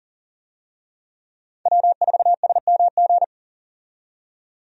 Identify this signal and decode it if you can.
Morse code, heard as a keyed tone, about 37 wpm, 700 Hz. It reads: W4SMG
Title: USA Call Signs - 5 Characters - 40wpm